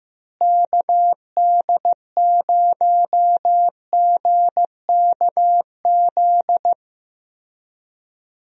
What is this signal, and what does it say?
Morse code, heard as a keyed tone, about 15 wpm, 700 Hz. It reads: KD0GKZ
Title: USA Call Signs - 6 Characters - Repeated 3x - 15wpm